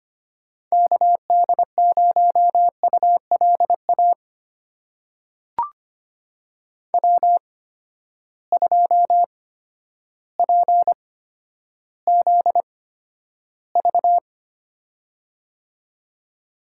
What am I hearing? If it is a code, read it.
Morse code, heard as a keyed tone, about 25 wpm, spaced as for 11 wpm, 700 Hz. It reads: KD0ULA W2PZV
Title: USA Call Signs - 8x Character Spacing - 25wpm